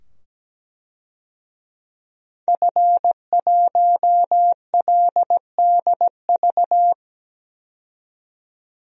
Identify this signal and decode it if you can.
Morse code, heard as a keyed tone, about 17 wpm, 700 Hz. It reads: F1LDV
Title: DXCC-100 Call Signs - 5 Characters - 17wpm